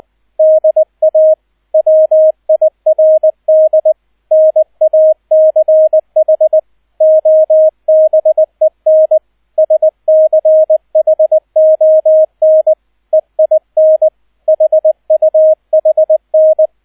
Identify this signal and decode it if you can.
Morse code, heard as a keyed tone, about 20 wpm, 630 Hz, with radio noise. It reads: DA WIRD NACH OBEN SCHON EIN HUHN
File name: 002 Zweiter Streich 20wpm